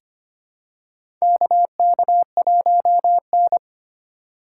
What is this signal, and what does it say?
Morse code, heard as a keyed tone, about 25 wpm, 700 Hz. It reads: KK1N
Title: USA Call Signs - 4 Characters - 3x Character Spacing - 25wpm